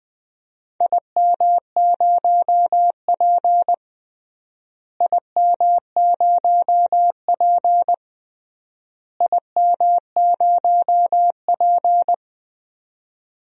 Morse code, heard as a keyed tone, about 20 wpm, 700 Hz. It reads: IM0P IM0P IM0P
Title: DXCC-100 Call Signs - 4 Characters - Repeated 3x - 20wpm